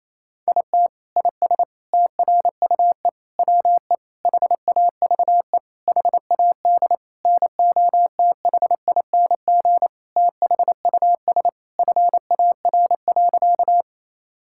Morse code, heard as a keyed tone, about 28 wpm, 700 Hz. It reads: IT IS TRUE WE HAVE HAD NOTHING THUS FAR.
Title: Sentences from Top 1000 Words - Review Part 5 - 28wpm